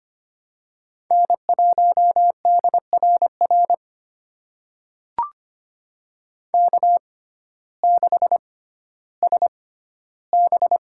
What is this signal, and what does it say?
Morse code, heard as a keyed tone, about 25 wpm, spaced as for 14 wpm, 700 Hz. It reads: N1DRR K6SB
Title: USA Call Signs - 6x Character Spacing - 25wpm